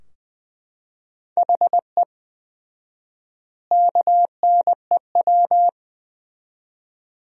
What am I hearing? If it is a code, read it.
Morse code, heard as a keyed tone, about 20 wpm, 700 Hz. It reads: HE KNEW
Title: 4x Word-Spacing - Sentences from Top 500 Words - 20wpm